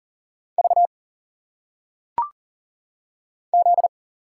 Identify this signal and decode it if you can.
Morse code, heard as a keyed tone, about 39 wpm, 700 Hz. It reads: V Z
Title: Single Letters - Phonetic Pronunciation 40wpm